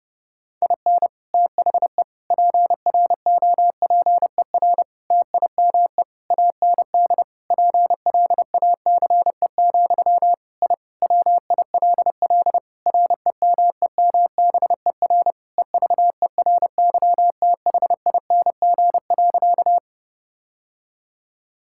Morse code, heard as a keyed tone, about 30 wpm, 700 Hz. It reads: IN THE PROPER TIME AND PLACE, I WILL REMEMBER EVERYTHING.
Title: Sentences from Top 1000 Words - Review Part 5 - 30wpm